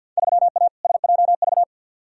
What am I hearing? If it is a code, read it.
Morse code, heard as a keyed tone, about 52 wpm, 700 Hz. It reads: 3A SJV
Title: Field Day Exchanges Encore - 50wpm